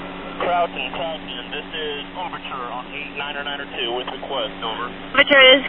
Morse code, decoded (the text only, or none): none